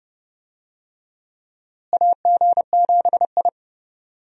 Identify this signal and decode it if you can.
Morse code, heard as a keyed tone, about 31 wpm, 700 Hz. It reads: AG7I
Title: USA Call Signs - 4 Characters - Repeated 3x - 30wpm